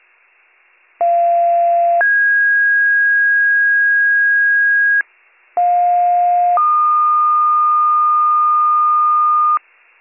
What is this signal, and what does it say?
Two-tone page, heard as a one-second tone, then a three-second tone, about 690 then 1650 Hz, about 710 then 1220 Hz, with noise. It is heard from 1.0 to 9.6 s.